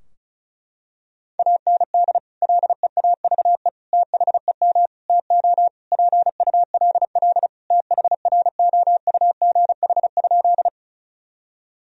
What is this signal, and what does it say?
Morse code, heard as a keyed tone, about 36 wpm, 700 Hz. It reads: AND LEAVE THEM TO PULL THROUGH?
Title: Sentences from Top 1000 Words - Review Part 2 - 35wpm